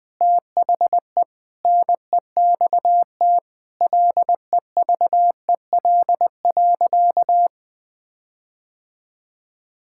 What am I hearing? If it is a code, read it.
Morse code, heard as a keyed tone, about 21 wpm, 700 Hz. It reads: THE NEXT LEVEL.
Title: Sentences with 7-Word Phrases - 20wpm